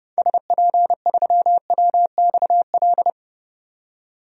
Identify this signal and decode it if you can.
Morse code, heard as a keyed tone, about 30 wpm, 700 Hz. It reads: SP3WXL
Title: DXCC-100 Call Signs - 6 Characters - 30wpm